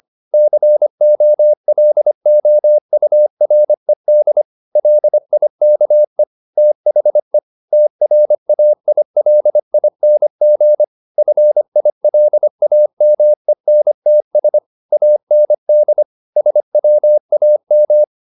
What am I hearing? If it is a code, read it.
Morse code, heard as a keyed tone, about 25 wpm, 600 Hz. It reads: COLOURED LIKE THE TRAILING FILAMENTS AND SWAM
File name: the old man and the sea part 1 25wpm